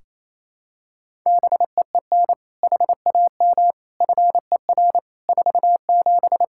Morse code, heard as a keyed tone, about 27 wpm, 700 Hz. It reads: BEEN HAM FER 47
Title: Practice QSO - Exchange 3 - More Info - 28wpm